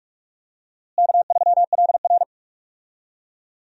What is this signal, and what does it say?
Morse code, heard as a keyed tone, about 46 wpm, 700 Hz. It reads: K3LR
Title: USA Call Signs - 4 Characters - Repeated 3x - 45wpm